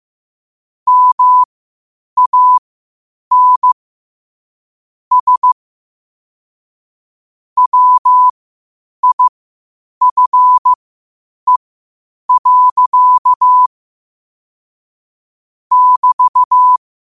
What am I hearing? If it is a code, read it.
Morse code, heard as a keyed tone, about 15 wpm, spaced as for 8 wpm, 1000 Hz. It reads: MAN S WIFE. <BT>
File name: qotdfiles / 8wpm